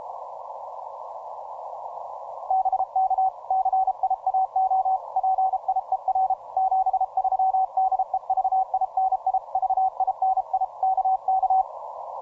Morse code, heard as a keyed tone, about 32 wpm, 760 Hz. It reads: BK CIAO PIER 73 DE VINI VINI KK
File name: simple-file-list / 9X0T CW 10M